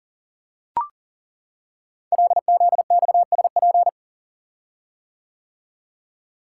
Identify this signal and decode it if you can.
Morse code, heard as a keyed tone, about 39 wpm, 700 Hz. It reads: LZXSP
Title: Code Groups - 5 Letters - 40wpm